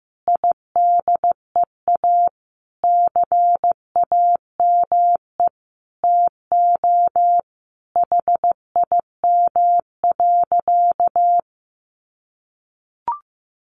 Morse code, heard as a keyed tone, about 15 wpm, 700 Hz. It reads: IDEA CAME TO HIM.
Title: Sentences from Top 1000 Words 15wpm